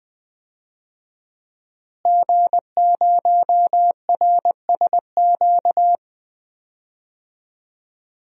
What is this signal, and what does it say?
Morse code, heard as a keyed tone, about 21 wpm, 700 Hz. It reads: G0RSQ
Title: DXCC-100 Call Signs - 5 Characters - Repeated 3x - 20wpm